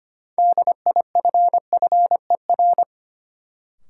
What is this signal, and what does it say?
Morse code, heard as a keyed tone, about 25 wpm, 700 Hz. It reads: DIFFER
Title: Sentences from Top 1100 Words - New Word Drill Alt - 25wpm